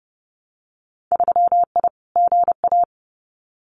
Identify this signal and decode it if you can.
Morse code, heard as a keyed tone, about 30 wpm, 700 Hz. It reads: 3I GA
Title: Winter Field Day Exchanges 30wpm